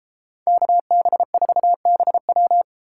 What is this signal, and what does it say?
Morse code, heard as a keyed tone, about 32 wpm, 700 Hz. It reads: KB4BW